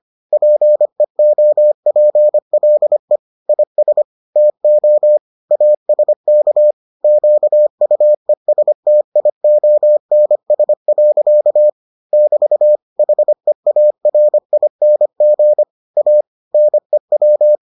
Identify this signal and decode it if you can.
Morse code, heard as a keyed tone, about 25 wpm, 600 Hz. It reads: PEOPLE IS TO ASK QUESTIONS. <BT> HEARING A NEW